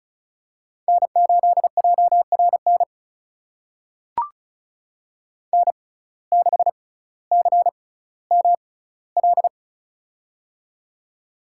Morse code, heard as a keyed tone, about 34 wpm, spaced as for 18 wpm, 700 Hz. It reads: N8JRN N6CML
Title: USA Call Signs - 5 Characters - 6x Character Spacing - 35wpm